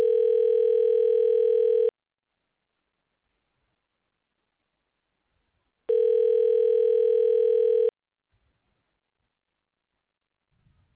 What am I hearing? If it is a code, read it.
Ringback tone (North America), 440 + 480 Hz.